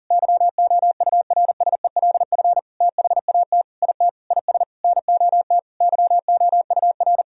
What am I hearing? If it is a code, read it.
Morse code, heard as a keyed tone, about 40 wpm, 700 Hz. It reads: YOURSELF THAT IT IS NOT YOUR
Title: Sentences from Top 1000 Words - Review Part 1 - 40wpm